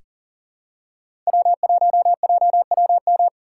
Morse code, heard as a keyed tone, about 40 wpm, 700 Hz. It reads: W1JWM
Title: USA Call Signs - 5 Characters - 4x Character Spacing - 40wpm